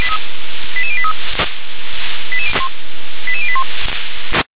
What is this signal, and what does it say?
Five-tone selective call (ZVEI), heard as quick runs of five tones, tones of about 70 ms. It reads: 88083 88081